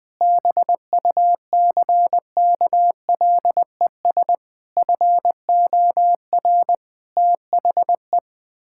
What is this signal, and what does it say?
Morse code, heard as a keyed tone, about 21 wpm, 700 Hz. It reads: BUCKLES FOR THE
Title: Pangram Challenge v2 - 20wpm